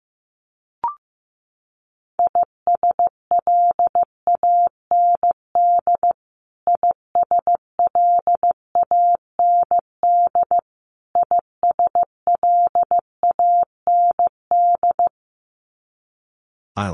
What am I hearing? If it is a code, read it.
Morse code, heard as a keyed tone, about 15 wpm, 700 Hz. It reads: ISLAND ISLAND ISLAND
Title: Top 500 Words Repeated 15wpm